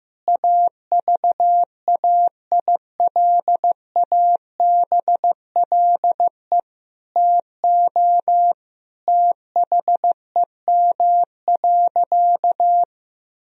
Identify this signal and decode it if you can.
Morse code, heard as a keyed tone, about 15 wpm, 700 Hz. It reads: AVAILABLE TO THEM.
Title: Sentences from Top 1000 Words - Review Part 3 - 15wpm